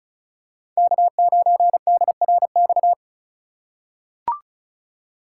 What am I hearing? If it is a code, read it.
Morse code, heard as a keyed tone, about 36 wpm, 700 Hz. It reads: K9DRX
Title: USA Call Signs - 2x Character Spacing - 35wpm